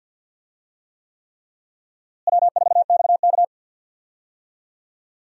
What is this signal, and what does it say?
Morse code, heard as a keyed tone, about 50 wpm, 700 Hz. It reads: W4XK
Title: USA Call Signs - 4 Characters - Repeated 3x - 50wpm